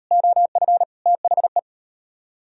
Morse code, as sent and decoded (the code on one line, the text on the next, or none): --- ..-. / - .... .
OF THE